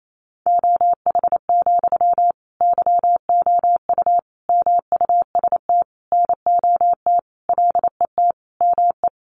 Morse code, heard as a keyed tone, about 28 wpm, 700 Hz. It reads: OH, YOU MUST NOT LET ME
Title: Sentences from Top 1100 Words 28wpm